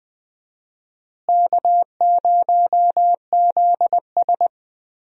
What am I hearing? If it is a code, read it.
Morse code, heard as a keyed tone, about 20 wpm, 700 Hz. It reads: K0ZS
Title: USA Call Signs - 4 Characters - Repeated 2x - 20wpm